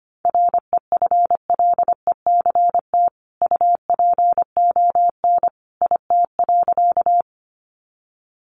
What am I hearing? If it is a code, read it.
Morse code, heard as a keyed tone, about 25 wpm, 700 Hz. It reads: REFLECT UPON IT.
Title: Sentences from Top 1500 Words 25wpm